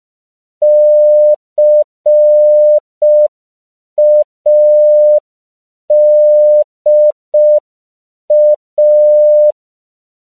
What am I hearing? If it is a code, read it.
Morse code, heard as a keyed tone, about 5 wpm, 600 Hz. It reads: CADA
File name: 5PPM